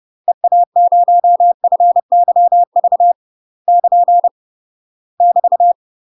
Morse code, heard as a keyed tone, about 30 wpm, 700 Hz. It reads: EA0FYV <KN> <BT>
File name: UK MASTER QSOs TRAINING / 30WPM / UK MASTER QSOs 0029